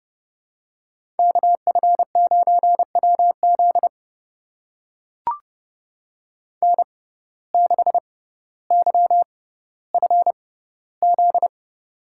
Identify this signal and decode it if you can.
Morse code, heard as a keyed tone, about 30 wpm, spaced as for 15 wpm, 700 Hz. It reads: KF9WZ N6YFZ
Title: USA Call Signs - 5 Characters - 6x Character Spacing - 30wpm